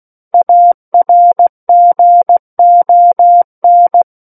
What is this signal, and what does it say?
Morse code, heard as a keyed tone, about 16 wpm, 700 Hz. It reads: ARGON
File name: Book One / 07-Chapter 7 - 16 WPM